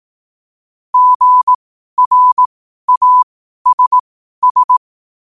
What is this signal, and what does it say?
Morse code, heard as a keyed tone, about 18 wpm, spaced as for 12 wpm, 1000 Hz. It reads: GRASS
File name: qotdfiles / 12wpm